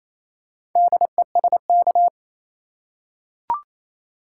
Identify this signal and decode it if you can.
Morse code, heard as a keyed tone, about 28 wpm, 700 Hz. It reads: DESK
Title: Four Letter Words - 28wpm